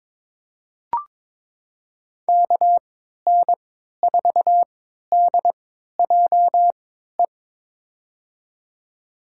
Morse code, heard as a keyed tone, about 22 wpm, spaced as for 13 wpm, 700 Hz. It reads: KN4DJE
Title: USA Call Signs - 3x Character Spacing - 22wpm